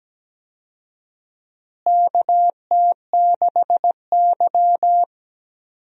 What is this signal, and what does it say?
Morse code, heard as a keyed tone, about 17 wpm, 700 Hz. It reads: KT6Y
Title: USA Call Signs - 4 Characters - Repeated 3x - 17wpm